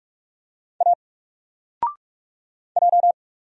Morse code, heard as a keyed tone, about 44 wpm, 700 Hz. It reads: A J
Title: Single Letter-Number 45wpm